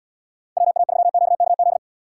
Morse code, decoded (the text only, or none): WA3QKZ